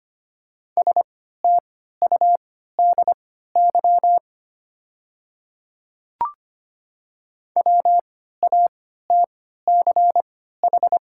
Morse code, heard as a keyed tone, about 25 wpm, spaced as for 14 wpm, 700 Hz. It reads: STUDY WATCH
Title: 3x Character Spacing - Top 300 Words - 25wpm